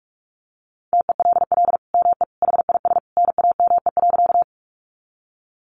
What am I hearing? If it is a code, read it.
Morse code, heard as a keyed tone, about 44 wpm, 700 Hz. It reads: TELL ME HIS NAME.
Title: Sentences from Top 200 Words 45wpm